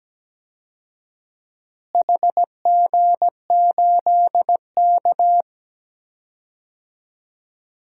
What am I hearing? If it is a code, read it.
Morse code, heard as a keyed tone, about 17 wpm, 700 Hz. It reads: HG8K
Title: DXCC-100 Call Signs - 4 Characters - Repeated 3x - 17wpm